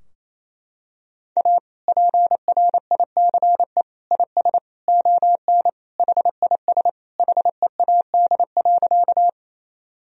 Morse code, heard as a keyed tone, about 28 wpm, 700 Hz. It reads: A PRICE IS ON HIS HEAD.
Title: Sentences from Top 1000 Words - Review Part 4 - 28wpm